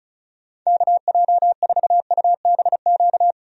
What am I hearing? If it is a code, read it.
Morse code, heard as a keyed tone, about 34 wpm, 700 Hz. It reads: KJ4UBQ
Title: USA Call Signs - 6 Characters - 35wpm